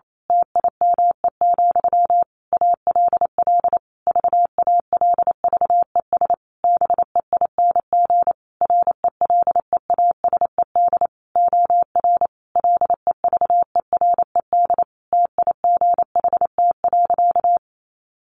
Morse code, heard as a keyed tone, about 28 wpm, 700 Hz. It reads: TIME, ALL VALVES BEING RELEASED OR LEVERED TIGHT.
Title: A Sound Of Thunder 28wpm